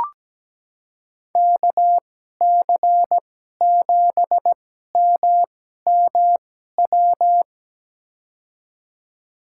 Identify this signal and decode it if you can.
Morse code, heard as a keyed tone, about 17 wpm, spaced as for 12 wpm, 700 Hz. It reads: KC7MMW
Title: USA Call Signs - 6 Characters - 2x Character Spacing - 17wpm